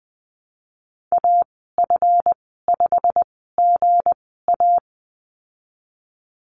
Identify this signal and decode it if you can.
Morse code, heard as a keyed tone, about 20 wpm, spaced as for 14 wpm, 700 Hz. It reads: AF5GA
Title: 161 Lesson 26g - Letter g - Call Signs 20wpm